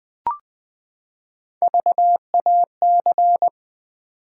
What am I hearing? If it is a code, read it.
Morse code, heard as a keyed tone, about 21 wpm, 700 Hz. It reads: VAC
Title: Build A Word - 3 Letters At A Time - 6-Letter Words - 20wpm